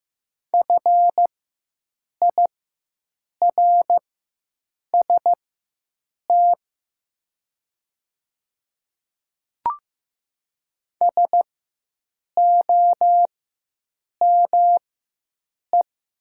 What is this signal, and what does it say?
Morse code, heard as a keyed tone, about 15 wpm, spaced as for 7 wpm, 700 Hz. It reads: FIRST SOME
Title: 4x Character Spacing - Top 100 Words - 15wpm